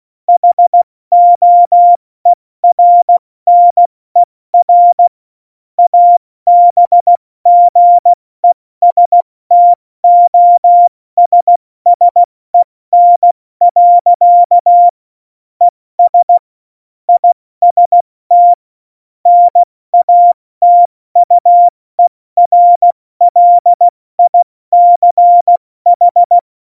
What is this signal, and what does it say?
Morse code, heard as a keyed tone, about 16 wpm, 700 Hz. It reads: HOERNER ABGESTOSSEN. ES IST NATUERLICH
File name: Morsekurs / Witze / Witz 14 Juli 2020, 70 F80